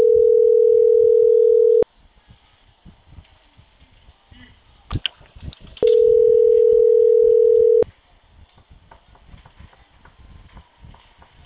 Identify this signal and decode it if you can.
Ringback tone (North America), 440 + 480 Hz.